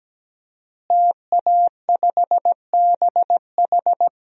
Morse code, heard as a keyed tone, about 17 wpm, 700 Hz. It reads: TA5BH
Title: DXCC-100 Call Signs - 5 Characters - Repeated 2x - 17wpm